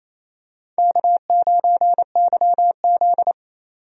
Morse code, heard as a keyed tone, about 28 wpm, 700 Hz. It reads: K9YZ
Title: USA Call Signs - 4 Characters - 28wpm